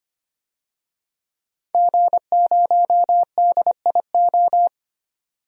Morse code, heard as a keyed tone, about 25 wpm, 700 Hz. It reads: G0DIO
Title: DXCC-100 Call Signs - 5 Characters - 25wpm